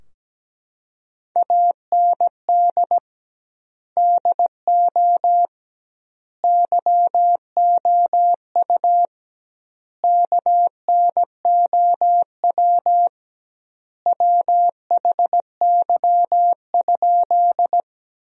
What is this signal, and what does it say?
Morse code, heard as a keyed tone, about 17 wpm, 700 Hz. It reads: AND DO YOU KNOW WHY?
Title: 2x Word-Spacing - Sentences from Top 200 Words - 17wpm